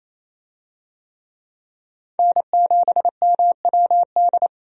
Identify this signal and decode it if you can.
Morse code, heard as a keyed tone, about 28 wpm, 700 Hz. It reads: N7MWD
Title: USA Call Signs - 5 Characters - 28wpm